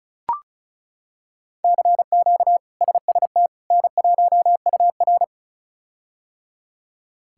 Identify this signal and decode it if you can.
Morse code, heard as a keyed tone, about 34 wpm, 700 Hz. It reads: CQ SST N1UR
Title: K1USN SST QSO Practice - 35wpm